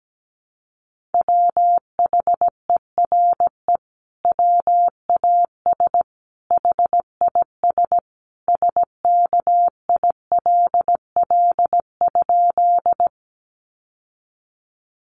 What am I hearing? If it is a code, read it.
Morse code, heard as a keyed tone, about 17 wpm, 700 Hz. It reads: WHERE WAS HIS SKILL?
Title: Sentences from Top 1200 Words 17wpm